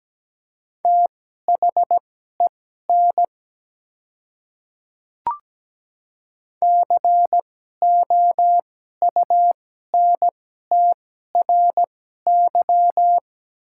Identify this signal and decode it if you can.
Morse code, heard as a keyed tone, about 17 wpm, spaced as for 12 wpm, 690 Hz. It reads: THEN COUNTRY
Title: 2x Character Spacing - Top 300 Words - 17wpm